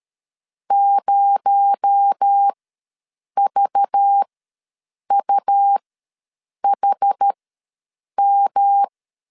Morse code, heard as a keyed tone, about 13 wpm, spaced as for 7 wpm, 800 Hz. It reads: 0VUHM
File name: LEZIONI / 8 YJ45 / 8 YJ45 ws 12wpm 3146